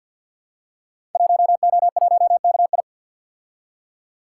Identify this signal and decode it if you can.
Morse code, heard as a keyed tone, about 47 wpm, 700 Hz. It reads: 1O1KI
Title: Code Groups - 5 Letter:Numbers - 50wpm